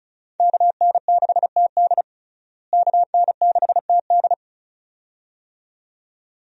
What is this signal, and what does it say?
Morse code, heard as a keyed tone, about 34 wpm, 700 Hz. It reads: KN6TD KN6TD
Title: USA Call Signs - 5 Characters - Repeated 2x - 35wpm